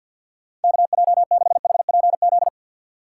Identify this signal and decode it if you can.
Morse code, heard as a keyed tone, about 52 wpm, 690 Hz. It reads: KJ6HPZ
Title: USA Call Signs - 6 Characters - 3x Character Spacing - 50wpm